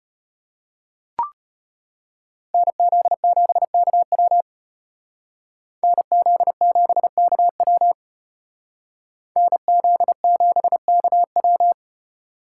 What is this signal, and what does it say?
Morse code, heard as a keyed tone, about 32 wpm, 700 Hz. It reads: NZ7KW NZ7KW NZ7KW